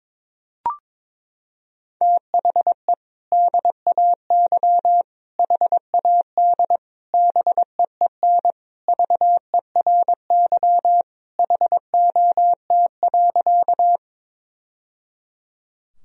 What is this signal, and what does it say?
Morse code, heard as a keyed tone, about 22 wpm, 700 Hz. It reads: THE DAY HAD BEEN VERY HOT.
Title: Sentences from Top 1000 Words - Review Part 3 - 22wpm